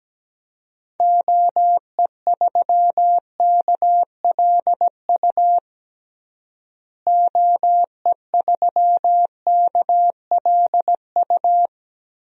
Morse code, heard as a keyed tone, about 17 wpm, 700 Hz. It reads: OE3KLU OE3KLU
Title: DXCC-100 Call Signs - 6 Characters - Repeated 3x - 17wpm